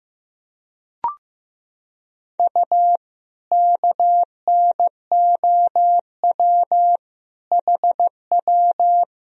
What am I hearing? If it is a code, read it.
Morse code, heard as a keyed tone, about 15 wpm, 700 Hz. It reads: U KNOW HW
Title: Sentences from Top 100 Words and Abbreviations 15wpm